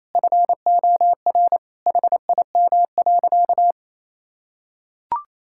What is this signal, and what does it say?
Morse code, heard as a keyed tone, about 28 wpm, 700 Hz. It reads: FOR HIM.
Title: Sentences from Top 1000 Words - Review Part 6 - 28wpm